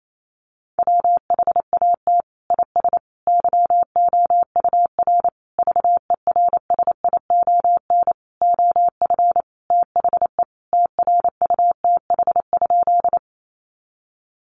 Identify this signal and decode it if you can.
Morse code, heard as a keyed tone, about 28 wpm, 700 Hz. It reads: WHAT IS YOUR VERSION OF THE TRUTH?
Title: Sentences from Top 1400 Words 28wpm